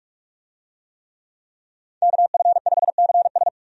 Morse code, heard as a keyed tone, about 46 wpm, 700 Hz. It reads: KF5CS
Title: USA Call Signs - 5 Characters - 45wpm